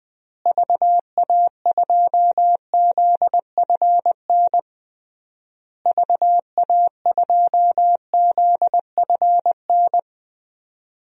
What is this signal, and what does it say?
Morse code, heard as a keyed tone, about 20 wpm, 700 Hz. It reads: VA2ZFN VA2ZFN
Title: DXCC-100 Call Signs - 6 Characters - Repeated 3x - 20wpm